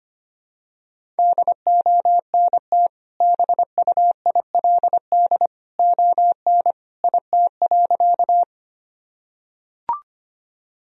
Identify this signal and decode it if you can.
Morse code, heard as a keyed tone, about 25 wpm, 700 Hz. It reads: DONT BUILD ON IT.
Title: Sentences from Top 1000 Words - Review Part 1 - 25wpm